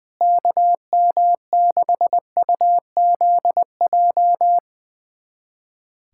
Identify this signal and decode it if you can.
Morse code, heard as a keyed tone, about 21 wpm, 700 Hz. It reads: KM6UZJ
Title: USA Callsign DE Callsign - 6 Characters - 20wpm